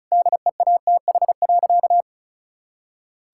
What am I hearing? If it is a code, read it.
Morse code, heard as a keyed tone, about 36 wpm, 700 Hz. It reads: DEATH.
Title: Sentences from Top 1600 Words 35wpm